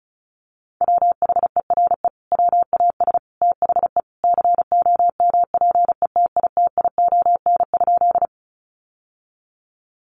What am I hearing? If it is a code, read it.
Morse code, heard as a keyed tone, about 34 wpm, 700 Hz. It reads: WHERE WAS THE COMPETITION?
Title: Sentences from Top 1000 Words 35wpm